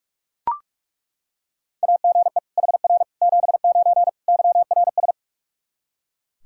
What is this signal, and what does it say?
Morse code, heard as a keyed tone, about 41 wpm, 700 Hz. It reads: AGE HR 79 YRS
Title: K1USN SST QSO Practice 45wpm